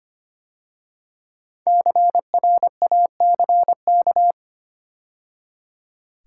Morse code, heard as a keyed tone, about 25 wpm, 700 Hz. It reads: CRACK
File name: 5 Letter Words - First 3 Letters Given - 25wpm